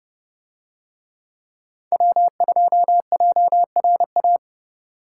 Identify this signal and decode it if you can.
Morse code, heard as a keyed tone, about 28 wpm, 700 Hz. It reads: W2JRA